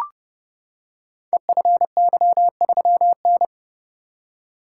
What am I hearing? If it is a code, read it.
Morse code, heard as a keyed tone, about 30 wpm, 700 Hz. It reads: EFY3N
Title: Code Groups - 5 Letter:Numbers - 30wpm